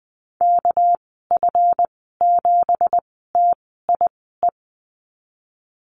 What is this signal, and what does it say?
Morse code, heard as a keyed tone, about 20 wpm, 700 Hz. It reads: K F 7 T I E
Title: 238 Lesson 38g - Letter q - Call Signs 20wpm